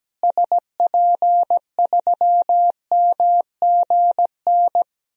Morse code, heard as a keyed tone, about 17 wpm, 700 Hz. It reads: SP3MGN
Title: DXCC-100 Call Signs - 6 Characters - Repeated 3x - 17wpm